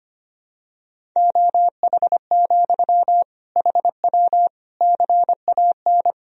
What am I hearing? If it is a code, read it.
Morse code, heard as a keyed tone, about 25 wpm, 700 Hz. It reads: OH, HW CAN
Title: Sentences from Top 500 Words and Abbreviations 25wpm